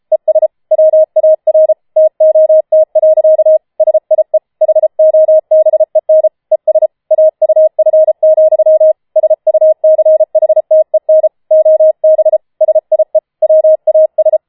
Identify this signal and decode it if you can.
Morse code, heard as a keyed tone, about 33 wpm, 630 Hz. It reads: ES WAR TOT. SIE HOBEN ES AUF, SUCHTEN OB SIE WAS
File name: Schneewittchen Teil5 34wpm